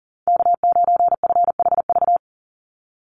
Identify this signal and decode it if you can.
Morse code, heard as a keyed tone, about 40 wpm, 700 Hz. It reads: K9FHV
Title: USA Call Signs 40wpm